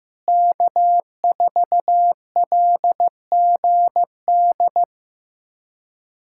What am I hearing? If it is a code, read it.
Morse code, heard as a keyed tone, about 15 wpm, 700 Hz. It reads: K4LGD
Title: USA Callsign DE Callsign - 5 Characters - 15wpm